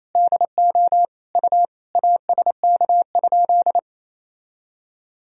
Morse code, heard as a keyed tone, about 28 wpm, 700 Hz. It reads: DO U ASK?
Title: Sentences from Top 200 Words and Abbreviations 28wpm